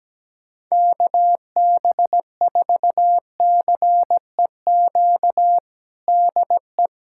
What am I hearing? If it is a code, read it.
Morse code, heard as a keyed tone, about 17 wpm, 700 Hz. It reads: KB4CEQ DE
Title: USA Callsign DE Callsign 17wpm